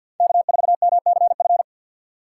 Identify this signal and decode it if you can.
Morse code, heard as a keyed tone, about 49 wpm, 700 Hz. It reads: K4MCF
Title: USA Call Signs - 5 Characters - 2x Character Spacing - 50wpm